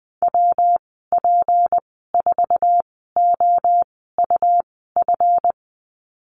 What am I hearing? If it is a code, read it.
Morse code, heard as a keyed tone, about 20 wpm, spaced as for 14 wpm, 700 Hz. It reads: WP4OUF
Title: 161 Lesson 26g - Letter g - Call Signs 20wpm